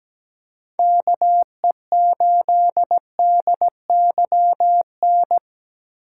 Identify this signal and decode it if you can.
Morse code, heard as a keyed tone, about 17 wpm, 700 Hz. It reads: KE8DYN
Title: USA Call Signs - 3x Character Spacing - 17wpm